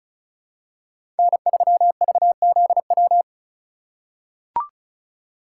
Morse code, heard as a keyed tone, about 36 wpm, 700 Hz. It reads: N3VZW
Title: USA Call Signs - 5 Characters - 4x Character Spacing - 35wpm